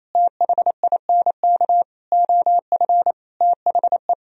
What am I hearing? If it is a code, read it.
Morse code, heard as a keyed tone, about 28 wpm, 700 Hz. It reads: THINK OF THE
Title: Sentences from Top 1000 Words - Review Part 1 - 28wpm